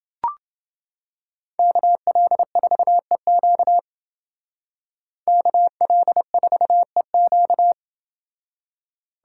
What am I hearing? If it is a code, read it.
Morse code, heard as a keyed tone, about 28 wpm, 700 Hz. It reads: KL4EQ KL4EQ